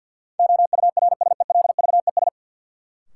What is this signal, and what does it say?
Morse code, heard as a keyed tone, about 53 wpm, 700 Hz. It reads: OURSELVES
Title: Sentences from Top 1600 Words - New Word Drill Alt - 50wpm